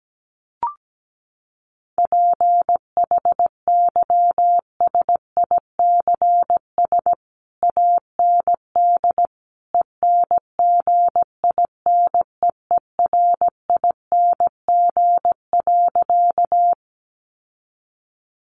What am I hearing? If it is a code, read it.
Morse code, heard as a keyed tone, about 17 wpm, 700 Hz. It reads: PHYSICS AND ENGINEERING.
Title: Sentences from Top 1500 Words 17wpm